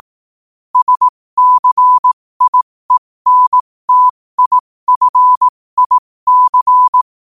Morse code, heard as a keyed tone, about 18 wpm, spaced as for 15 wpm, 1000 Hz. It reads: SCIENTIFIC
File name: qotdfiles / 15wpm